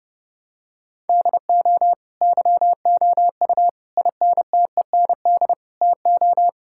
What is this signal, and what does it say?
Morse code, heard as a keyed tone, about 32 wpm, 700 Hz. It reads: DO YOU INTEND TO
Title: Sentences from Top 1600 Words 30wpm